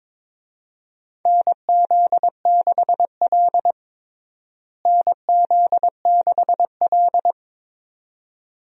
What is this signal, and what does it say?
Morse code, heard as a keyed tone, about 22 wpm, 700 Hz. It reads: NZ6L NZ6L
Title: USA Call Signs - 4 Characters - Repeated 2x - 22wpm